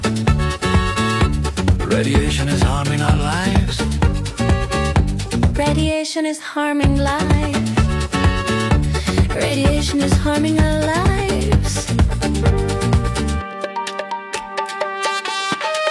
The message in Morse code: none